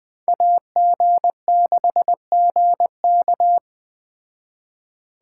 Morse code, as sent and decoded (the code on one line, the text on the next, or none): .- --. -.... --. -.-
AG6GK